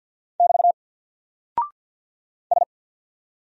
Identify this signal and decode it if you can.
Morse code, heard as a keyed tone, about 50 wpm, 700 Hz. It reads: <BT> S
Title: Single Character (Letter:Number:Punctuation:Prosign) 50wpm